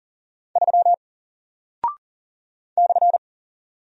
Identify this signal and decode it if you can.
Morse code, heard as a keyed tone, about 40 wpm, 700 Hz. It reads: 3 /
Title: Single Character (Letter:Number:Punctuation:Prosign) 40wpm